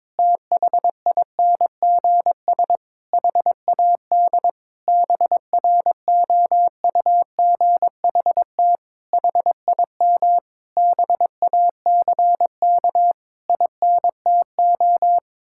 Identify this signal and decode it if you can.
Morse code, heard as a keyed tone, about 22 wpm, 700 Hz. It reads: THINGS HAD BROUGHT HIM BACK INTO
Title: Sentences from Top 1000 Words - Review Part 6 - 22wpm